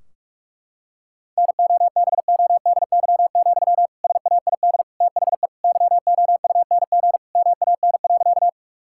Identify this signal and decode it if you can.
Morse code, heard as a keyed tone, about 46 wpm, 700 Hz. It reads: NOBODY, SAID THE YOUNG MAN.
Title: Sentences from Top 1600 Words 45wpm